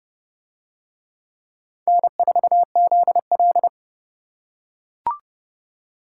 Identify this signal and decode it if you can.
Morse code, heard as a keyed tone, about 28 wpm, 700 Hz. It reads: N4ZL